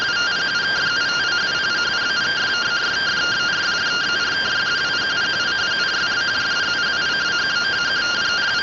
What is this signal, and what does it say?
Two-tone teleprinter signal, about 170 Hz shift.